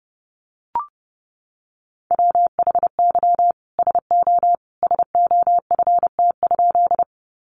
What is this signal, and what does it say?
Morse code, heard as a keyed tone, about 30 wpm, 700 Hz. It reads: WHY SO SOFT?
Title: Sentences from Top 1500 Words 30wpm